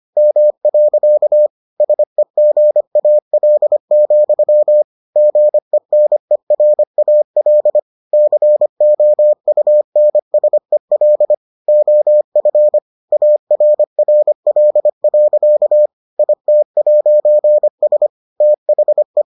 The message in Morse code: M. SEGAL, GENERAL COUNSEL OF ARRL. IT'S THE

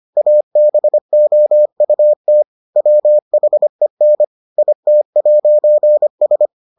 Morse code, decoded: ABOUT WHEN IT'S

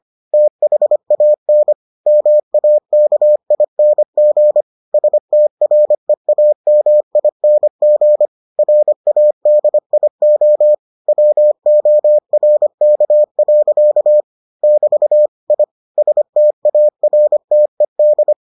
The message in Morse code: THAN MAKING STREAMING RADIO WORK. <BT> I STARTED

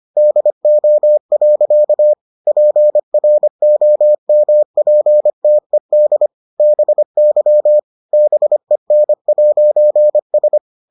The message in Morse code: DO. PROMPTED BY BEN'S

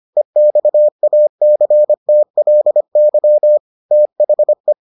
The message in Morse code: EXACTLY THE

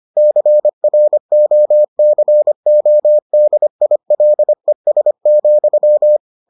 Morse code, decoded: CROCODILES,

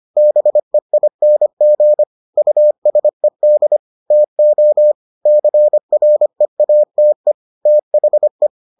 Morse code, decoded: BEING USED TO CREATE THE

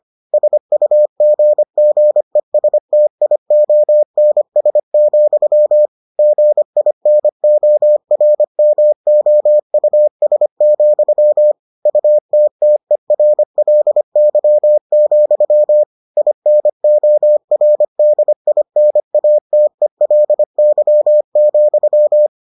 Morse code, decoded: SUGGESTIONS, GINORMOUS, UTTERLY, INORDINATELY,

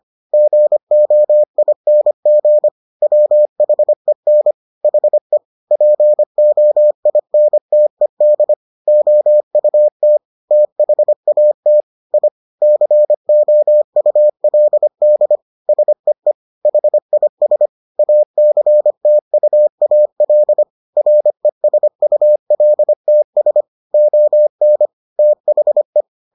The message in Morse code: GOING WHEN HE POINTED OUT THAT I COULD SEE HIS ACTUAL RESULTS ON THE

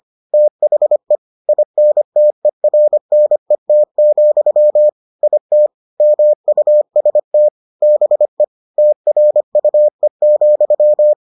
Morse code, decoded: THE INTERNET, IT MUST BE TRUE,

